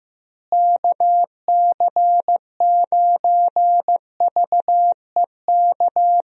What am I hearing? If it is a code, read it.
Morse code, heard as a keyed tone, about 15 wpm, 700 Hz. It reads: KC9VEK
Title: USA Call Signs - 6 Characters - Repeated 3x - 15wpm